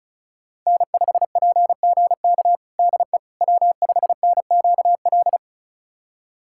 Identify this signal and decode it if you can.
Morse code, heard as a keyed tone, about 34 wpm, 700 Hz. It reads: N5PGK DE W5NQL
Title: USA Callsign DE Callsign - 5 Characters - 35wpm